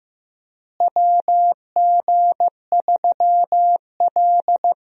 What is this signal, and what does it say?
Morse code, heard as a keyed tone, about 15 wpm, 700 Hz. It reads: WG3L
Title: USA Call Signs - 4 Characters - Repeated 3x - 15wpm